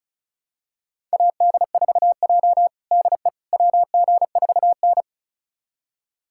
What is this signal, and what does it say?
Morse code, heard as a keyed tone, about 34 wpm, 700 Hz. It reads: AD4J DE WG4N
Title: USA Callsign DE Callsign - 4 Characters - 35wpm